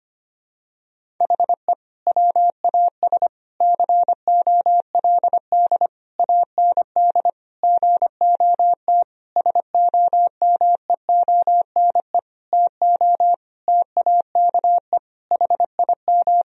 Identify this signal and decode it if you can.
Morse code, heard as a keyed tone, about 25 wpm, 700 Hz. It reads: HE WAS COLD AND GOT SOMEONE TO TAKE HIM
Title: Sentences from Top 1000 Words - Review Part 4 - 25wpm